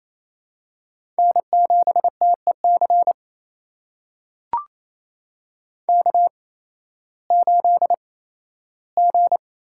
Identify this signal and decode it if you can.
Morse code, heard as a keyed tone, about 28 wpm, spaced as for 15 wpm, 700 Hz. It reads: N7TEC K8G
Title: USA Call Signs - 5 Characters - 8x Character Spacing - 28wpm